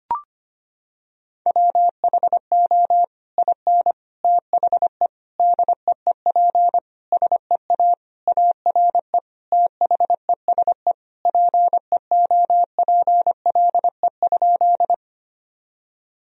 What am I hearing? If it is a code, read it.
Morse code, heard as a keyed tone, about 25 wpm, 700 Hz. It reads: WHO IN THE DEEP SEA ARE THESE PEOPLE?
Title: Sentences from Top 1000 Words - Review Part 3 - 25wpm